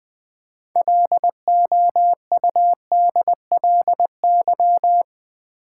Morse code, heard as a keyed tone, about 21 wpm, 700 Hz. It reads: LOUDLY
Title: 6 Letter Words - First 3 Letters Given - 20wpm